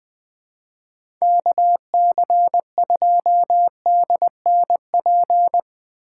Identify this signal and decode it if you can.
Morse code, heard as a keyed tone, about 20 wpm, 700 Hz. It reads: KC2DNP
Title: USA Call Signs - 2x Character Spacing - 20wpm